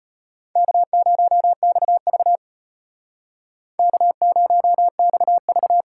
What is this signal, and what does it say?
Morse code, heard as a keyed tone, about 33 wpm, 700 Hz. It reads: K0XV K0XV